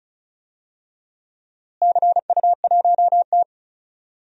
Morse code, heard as a keyed tone, about 34 wpm, 700 Hz. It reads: CU1T
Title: DXCC-100 Call Signs - 4 Characters - Repeated 3x - 35wpm